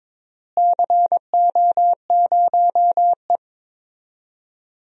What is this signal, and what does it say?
Morse code, heard as a keyed tone, about 22 wpm, 700 Hz. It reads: CO0E
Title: DXCC-100 Call Signs - 4 Characters - Repeated 3x - 22wpm